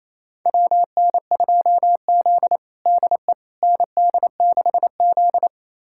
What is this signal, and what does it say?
Morse code, heard as a keyed tone, about 28 wpm, 700 Hz. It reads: WN2Z DE ND6Z
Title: USA Callsign DE Callsign - 4 Characters - 28wpm